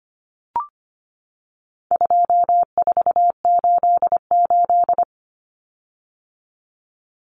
Morse code, heard as a keyed tone, about 25 wpm, 700 Hz. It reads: 2488
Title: Common Cut Numbers - 4-Digit 25wpm